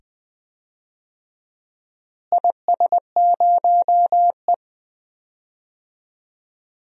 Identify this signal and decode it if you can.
Morse code, heard as a keyed tone, about 20 wpm, 700 Hz. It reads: IS0E
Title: DXCC-100 Call Signs - 4 Characters - Repeated 2x - 20wpm